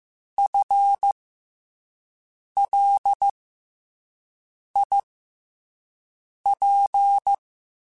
Morse code, heard as a keyed tone, about 15 wpm, spaced as for 5 wpm, 800 Hz. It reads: FLIP
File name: FL Runs / FL Runs mp3s / FL Runs @5